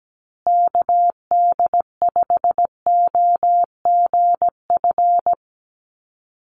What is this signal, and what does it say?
Morse code, heard as a keyed tone, about 17 wpm, 700 Hz. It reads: KD5OGF
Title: USA Call Signs Repeated 2x 17wpm